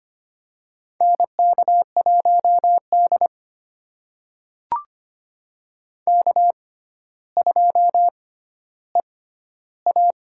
Morse code, heard as a keyed tone, about 25 wpm, spaced as for 13 wpm, 700 Hz. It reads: NK1D K2EA
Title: USA Call Signs - 4 Characters - 6x Character Spacing - 25wpm